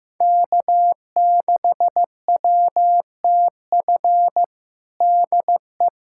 Morse code, heard as a keyed tone, about 16 wpm, 700 Hz. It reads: K6WTF DE
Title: USA Callsign DE Callsign - 5 Characters - 15wpm